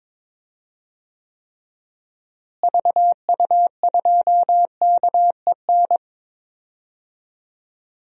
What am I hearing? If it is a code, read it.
Morse code, heard as a keyed tone, about 22 wpm, 700 Hz. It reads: VU2KEN
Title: DXCC-100 Call Signs - 6 Characters - Repeated 3x - 22wpm